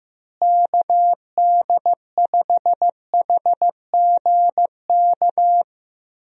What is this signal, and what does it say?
Morse code, heard as a keyed tone, about 16 wpm, 700 Hz. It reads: KD5HGK
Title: USA Call Signs - 6 Characters - Repeated 3x - 15wpm